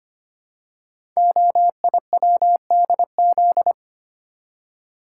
Morse code, heard as a keyed tone, about 25 wpm, 700 Hz. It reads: OIWDZ
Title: Code Groups - 5 Letters - 25wpm